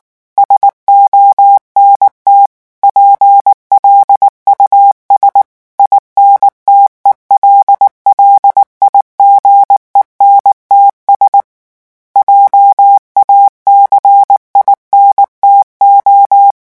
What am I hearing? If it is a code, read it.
Morse code, heard as a keyed tone, about 19 wpm, 800 Hz. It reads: SONT PLUS INTELLIGENTS JACINTO
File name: morse mp3 / revision9 citations humoristiques vitesse 18 mots minute